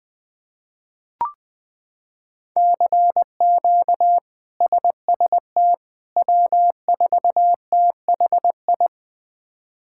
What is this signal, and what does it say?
Morse code, heard as a keyed tone, about 20 wpm, 700 Hz. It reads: CQ SST W4THI
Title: K1USN SST QSO Practice - 20wpm